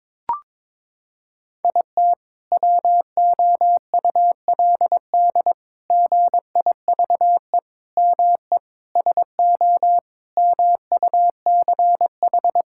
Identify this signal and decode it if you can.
Morse code, heard as a keyed tone, about 22 wpm, 700 Hz. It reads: IT WOULD GIVE ME SO MUCH
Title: Sentences from Top 1600 Words 22wpm